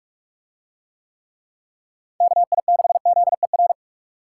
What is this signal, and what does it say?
Morse code, heard as a keyed tone, about 44 wpm, 700 Hz. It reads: KI6ZER